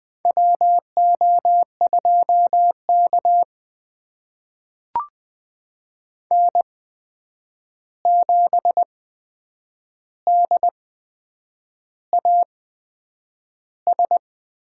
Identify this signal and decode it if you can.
Morse code, heard as a keyed tone, about 20 wpm, spaced as for 8 wpm, 700 Hz. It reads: WO2K N7DAS
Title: USA Call Signs - 8x Character Spacing - 20wpm